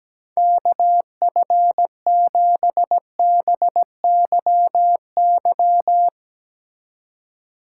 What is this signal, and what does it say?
Morse code, heard as a keyed tone, about 18 wpm, 700 Hz. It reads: KF7BYY
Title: Speed Racing - USA 2x3 Call Signs 17wpm